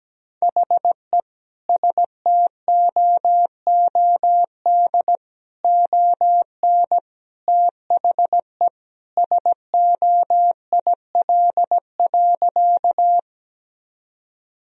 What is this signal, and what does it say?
Morse code, heard as a keyed tone, about 17 wpm, 700 Hz. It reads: HE STOOD ON THE SOIL.
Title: Sentences from Top 1000 Words - Review Part 4 - 17wpm